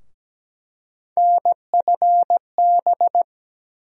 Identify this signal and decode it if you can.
Morse code, heard as a keyed tone, about 17 wpm, 700 Hz. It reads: NFB
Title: USA Call Sign Postfixes - 3 Letters 17wpm